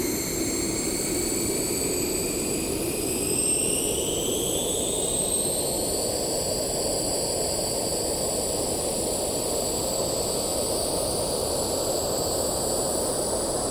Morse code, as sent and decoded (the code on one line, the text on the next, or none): none